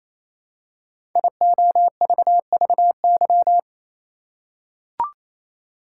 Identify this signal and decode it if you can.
Morse code, heard as a keyed tone, about 28 wpm, 700 Hz. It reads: IOVVY
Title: Code Groups - 5 Letter:Numbers - 28wpm